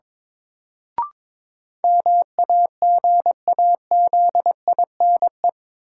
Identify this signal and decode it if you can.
Morse code, heard as a keyed tone, about 22 wpm, 700 Hz. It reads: MAGAZINE
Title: Sentences from Top 1000 Words - Review Part 7 - New Word Drill - 22wpm